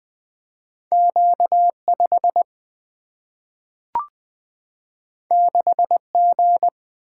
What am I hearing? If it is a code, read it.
Morse code, heard as a keyed tone, about 20 wpm, 700 Hz. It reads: Q 5 6 G
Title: Single-Letter Single-Number 20wpm